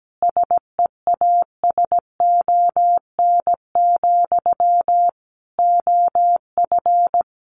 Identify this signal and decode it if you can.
Morse code, heard as a keyed tone, about 17 wpm, 700 Hz. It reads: SEASON, OF
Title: Sentences from Top 1400 Words 17wpm